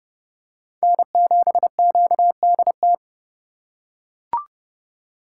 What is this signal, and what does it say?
Morse code, heard as a keyed tone, about 28 wpm, 700 Hz. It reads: N7QDT